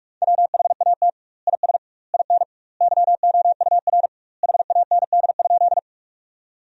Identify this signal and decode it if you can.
Morse code, heard as a keyed tone, about 46 wpm, 700 Hz. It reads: WHAT IS IN YOUR HAND?
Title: 2x Word-Spacing - Sentences from Top 200 Words - 45wpm